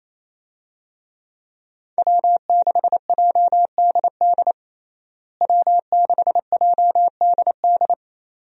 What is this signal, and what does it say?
Morse code, heard as a keyed tone, about 28 wpm, 700 Hz. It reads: W6JDD W6JDD
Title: USA Call Signs - 5 Characters - Repeated 2x - 28wpm